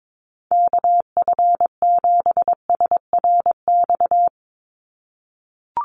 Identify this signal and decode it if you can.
Morse code, heard as a keyed tone, about 22 wpm, 700 Hz. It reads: KF7SRX
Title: USA Call Signs 22wpm